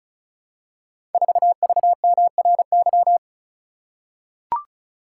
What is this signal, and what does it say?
Morse code, heard as a keyed tone, about 36 wpm, 700 Hz. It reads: 4VMRY
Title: Code Groups - 5 Letter:Numbers - 35wpm